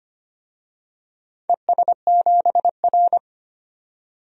Morse code, heard as a keyed tone, about 25 wpm, 700 Hz. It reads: ES7R
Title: DXCC-100 Call Signs - 4 Characters - 25wpm